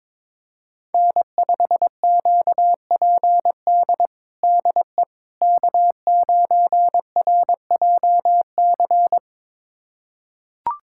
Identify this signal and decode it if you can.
Morse code, heard as a keyed tone, about 22 wpm, 700 Hz. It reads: N5QPD DE K9RJC
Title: USA Callsign DE Callsign - 5 Characters - 22wpm